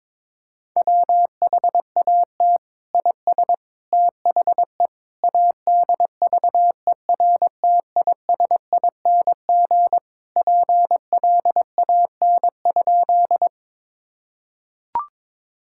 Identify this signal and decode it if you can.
Morse code, heard as a keyed tone, about 22 wpm, 700 Hz. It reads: WHAT IS THE ADVERTISING PLAN?
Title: Sentences from Top 1000 Words - Review Part 6 - 22wpm